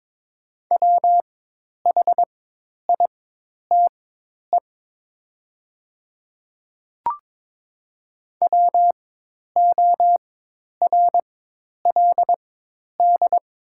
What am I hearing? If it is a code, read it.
Morse code, heard as a keyed tone, about 22 wpm, spaced as for 10 wpm, 690 Hz. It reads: WHITE WORLD
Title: 4x Character Spacing - Top 300 Words - 22wpm